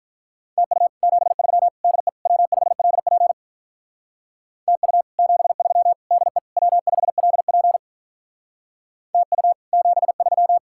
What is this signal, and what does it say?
Morse code, heard as a keyed tone, about 44 wpm, 700 Hz. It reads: TU 73 DE W5LP TU 73 DE W5LP TU 73